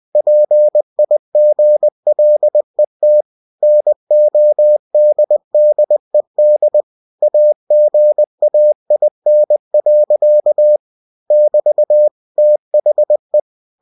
Morse code, heard as a keyed tone, about 20 wpm, 600 Hz. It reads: PIGLET NODDED AGAIN. <BT> THE
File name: cw-books / winnie-the-pooh / winnie-20wpm-page-0056